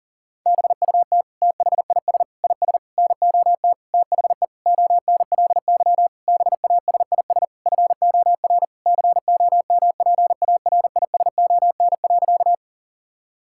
Morse code, heard as a keyed tone, about 40 wpm, 700 Hz. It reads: BUT THIS IS NOT THE ONLY BASIS FOR COMPARISON.
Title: Sentences from Top 1600 Words 40wpm